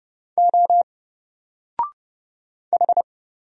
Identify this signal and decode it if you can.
Morse code, heard as a keyed tone, about 30 wpm, 700 Hz. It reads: O H
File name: Single-Letter-Number-30wpm